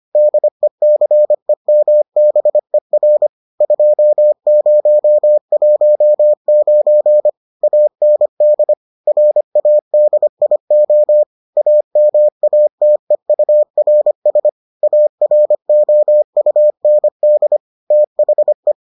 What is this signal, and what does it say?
Morse code, heard as a keyed tone, about 25 wpm, 600 Hz. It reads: DECEMBER 2019 AND RADIO AMATEURS AROUND THE